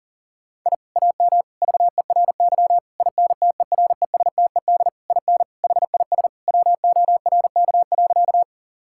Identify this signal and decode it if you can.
Morse code, heard as a keyed tone, about 40 wpm, 700 Hz. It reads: I AM VERY INTERESTED IN HIS WORK.
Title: Sentences from Top 1000 Words - Review Part 6 - 40wpm